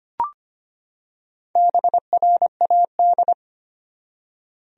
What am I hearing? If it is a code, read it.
Morse code, heard as a keyed tone, about 25 wpm, 700 Hz. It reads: BRAD
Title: USA Common Names - Max 4 Letters - 25wpm